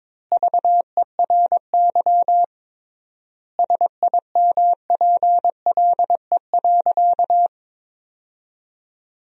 Morse code, heard as a keyed tone, about 21 wpm, 700 Hz. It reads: VERY SIMPLE.
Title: 3x Word-Spacing - Sentences from Top 400 Words - 22wpm